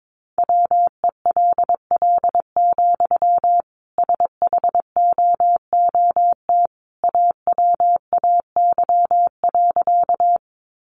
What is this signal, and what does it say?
Morse code, heard as a keyed tone, about 22 wpm, 700 Hz. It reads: WELL, SHOOT AWAY.
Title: Sentences from Top 1200 Words 22wpm